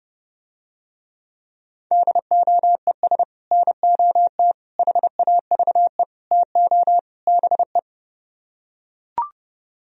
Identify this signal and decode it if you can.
Morse code, heard as a keyed tone, about 30 wpm, 700 Hz. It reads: DOES NOT HAVE TO BE